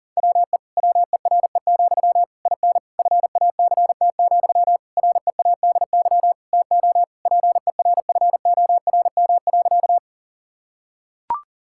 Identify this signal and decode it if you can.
Morse code, heard as a keyed tone, about 40 wpm, 700 Hz. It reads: WE WERE, IN FACT, READY TO PERFORM.
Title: Sentences from Top 1000 Words - Review Part 5 - 40wpm